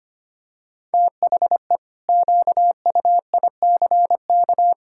Morse code, heard as a keyed tone, about 25 wpm, 700 Hz. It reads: THE QUICK
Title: Pangram Challenge v2 - 25wpm